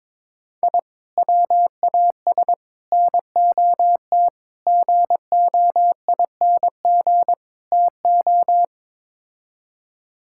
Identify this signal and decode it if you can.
Morse code, heard as a keyed tone, about 22 wpm, 700 Hz. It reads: I WAS NOT GOING TO
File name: practice / 22wpm / Build A Phrase - 5-Word Phrases - 22wpm